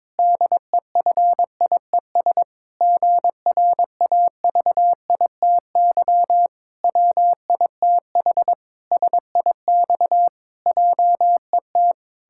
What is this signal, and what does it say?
Morse code, heard as a keyed tone, about 22 wpm, 700 Hz. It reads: DEFIES GRAVITY WITH SIX JET
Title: Pangram Challenge v2 - 22wpm